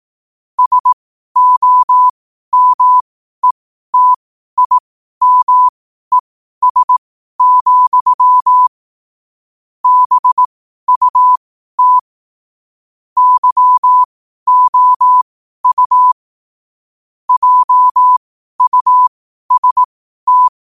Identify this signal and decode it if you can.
Morse code, heard as a keyed tone, about 18 wpm, spaced as for 12 wpm, 1000 Hz. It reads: SOMETIMES, BUT YOU JUST
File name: qotdfiles / 12wpm